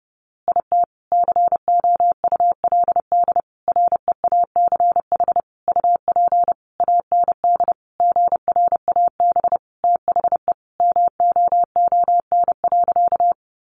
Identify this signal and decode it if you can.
Morse code, heard as a keyed tone, about 30 wpm, 700 Hz. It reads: IT COULD REACH UP AND GRAB THE MOON.
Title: A Sound Of Thunder 30wpm